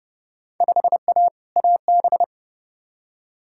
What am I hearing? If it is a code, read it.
Morse code, heard as a keyed tone, about 30 wpm, 700 Hz. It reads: 5A AB
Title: Field Day Exchanges Encore - 30wpm